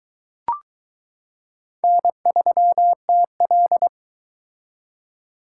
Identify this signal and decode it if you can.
Morse code, heard as a keyed tone, about 22 wpm, 700 Hz. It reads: N3TL